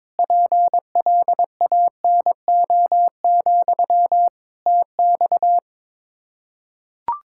Morse code, heard as a keyed tone, about 22 wpm, 700 Hz. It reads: PLANO, TX
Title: Practice QSO - Exchange 1 - Beginning - 22wpm